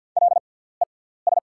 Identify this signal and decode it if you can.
Morse code, heard as a keyed tone, about 48 wpm, spaced as for 17 wpm, 690 Hz. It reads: LES
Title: 6x Character Spacing - Top 500 Words - 50wpm